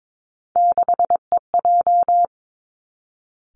Morse code, heard as a keyed tone, about 22 wpm, 700 Hz. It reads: 6EJ
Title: Three Challenging Letter-Number Combinations 22wpm